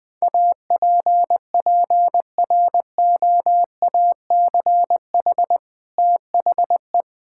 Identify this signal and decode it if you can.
Morse code, heard as a keyed tone, about 20 wpm, 700 Hz. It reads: APPROACH THE
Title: Sentences from Top 1000 Words - Review Part 6 - 20wpm